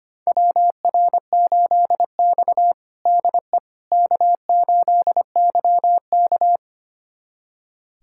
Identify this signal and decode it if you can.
Morse code, heard as a keyed tone, about 25 wpm, 700 Hz. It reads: WR8X DE K8YK
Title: USA Callsign DE Callsign - 4 Characters - 25wpm